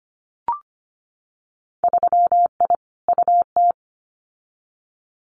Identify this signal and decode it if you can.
Morse code, heard as a keyed tone, about 25 wpm, 700 Hz. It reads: 3I UT
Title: Winter Field Day Exchanges Encore 25wpm